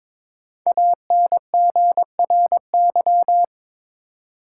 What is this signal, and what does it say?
Morse code, heard as a keyed tone, about 22 wpm, 700 Hz. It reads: ANGRY
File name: 5 Letter Words - First 3 Letters Given - 22wpm